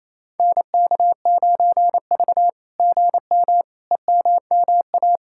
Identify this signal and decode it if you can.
Morse code, heard as a keyed tone, about 27 wpm, 700 Hz. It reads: NK9V GM EMMA
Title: SOTA QSO Practice 28wpm